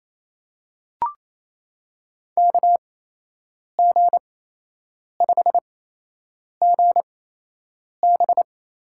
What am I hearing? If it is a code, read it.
Morse code, heard as a keyed tone, about 28 wpm, spaced as for 8 wpm, 700 Hz. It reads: KG5GB
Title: USA Call Signs - 8x Character Spacing - 28wpm